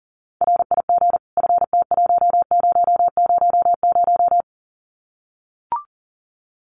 Morse code, heard as a keyed tone, about 40 wpm, 700 Hz. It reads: RIG FT1000
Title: Common QSO Phrases 40wpm